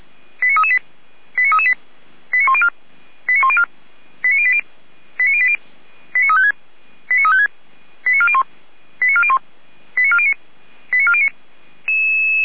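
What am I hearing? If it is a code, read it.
Five-tone selective call (ZVEI), heard as quick runs of five tones, tones of about 70 ms. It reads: 89338 89338 89104 89104 89080 89080 89356 89356 89401 89401 89409 89409
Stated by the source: ZVEI five-tone call received by SDR.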